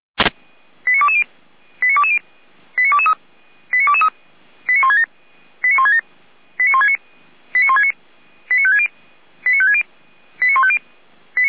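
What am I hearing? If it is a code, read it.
Five-tone selective call (ZVEI), heard as quick runs of five tones, tones of about 70 ms. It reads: 89220 89220 89223 89223 89167 89167 89169 89169 89560 89560 89150
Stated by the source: ZVEI five-tone call received on SDR by feuerwehr-orlamuende fire brigade.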